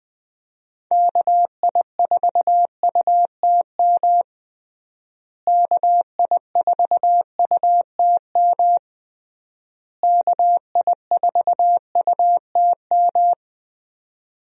Morse code, heard as a keyed tone, about 20 wpm, 700 Hz. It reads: KI4UTM KI4UTM KI4UTM
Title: USA Call Signs - 6 Characters - Repeated 3x - 20wpm